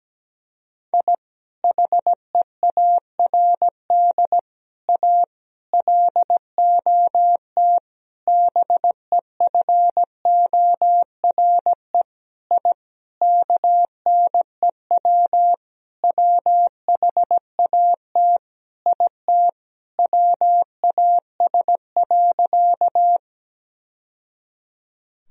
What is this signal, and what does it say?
Morse code, heard as a keyed tone, about 17 wpm, 700 Hz. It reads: I HEARD A LOT BEFORE I KNEW WHAT IT WAS.
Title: Sentences from Top 1000 Words - Review Part 3 - 17wpm